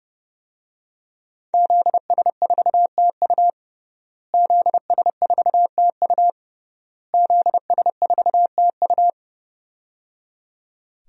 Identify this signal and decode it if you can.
Morse code, heard as a keyed tone, about 30 wpm, 700 Hz. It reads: ZS4TU ZS4TU ZS4TU
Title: DXCC-100 Call Signs - 5 Characters - Repeated 3x - 30wpm